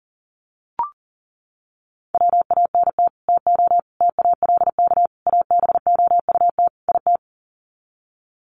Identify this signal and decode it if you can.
Morse code, heard as a keyed tone, about 40 wpm, 700 Hz. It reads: WANT TO TALK ABOUT IT
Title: Sets of 5 Words Encore 40wpm